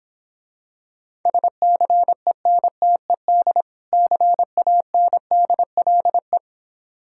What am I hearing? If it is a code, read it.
Morse code, heard as a keyed tone, about 26 wpm, 700 Hz. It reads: SCENTED CANDLE